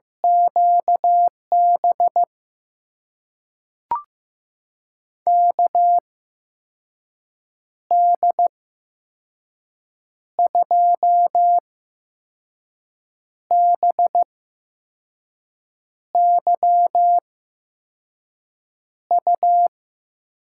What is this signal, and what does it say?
Morse code, heard as a keyed tone, about 15 wpm, spaced as for 5 wpm, 700 Hz. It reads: QB KD2BYU
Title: USA Call Signs - 8x Character Spacing - 15wpm